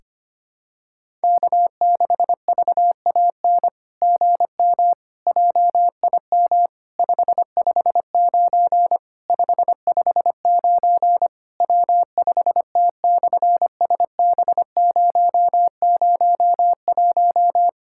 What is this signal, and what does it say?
Morse code, heard as a keyed tone, about 24 wpm, 700 Hz. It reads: K6VAN GM JIM 559 559 W5T/SB001
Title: SOTA QSO Practice 25wpm